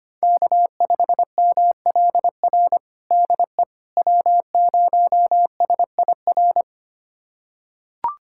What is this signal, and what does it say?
Morse code, heard as a keyed tone, about 25 wpm, 700 Hz. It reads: K5MLR DE W0SIR
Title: USA Callsign DE Callsign - 5 Characters - 25wpm